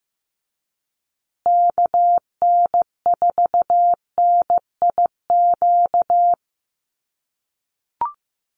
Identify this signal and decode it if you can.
Morse code, heard as a keyed tone, about 15 wpm, 700 Hz. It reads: KN4NIQ
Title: USA Call Signs Repeated 3x 15wpm